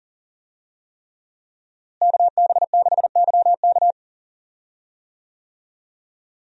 Morse code, heard as a keyed tone, about 40 wpm, 690 Hz. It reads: KB6YK
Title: USA Call Signs - 5 Characters - Repeated 3x - 40wpm